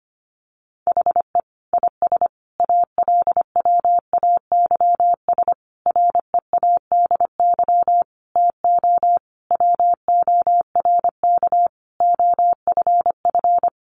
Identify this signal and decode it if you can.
Morse code, heard as a keyed tone, about 25 wpm, 700 Hz. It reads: HE IS ALWAYS READY TO WORK OFF
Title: Sentences from Top 1300 Words 25wpm